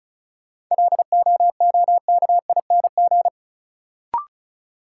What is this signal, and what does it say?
Morse code, heard as a keyed tone, about 36 wpm, 700 Hz. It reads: LOOKING
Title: Sentences from Top 1000 Words - Review Part 6 - New Word Drill - 35wpm